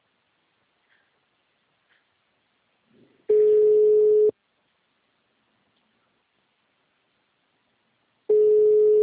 Ringback tone (Europe, 425 Hz).